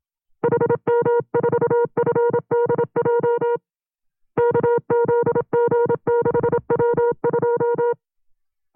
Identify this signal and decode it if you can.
Morse code, heard as a keyed tone, about 27 wpm, 470 Hz. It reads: HM4FDJ KZG6W2
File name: FreeFiles / CW-practice / 25WPM-side2